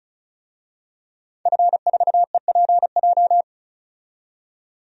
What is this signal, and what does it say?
Morse code, heard as a keyed tone, about 36 wpm, 700 Hz. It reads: F4EPJ
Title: DXCC-100 Call Signs - 5 Characters - 35wpm